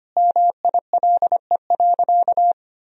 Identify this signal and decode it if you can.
Morse code, heard as a keyed tone, about 25 wpm, 700 Hz. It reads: MILE.
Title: Sentences from Top 1000 Words - Review Part 4 - 25wpm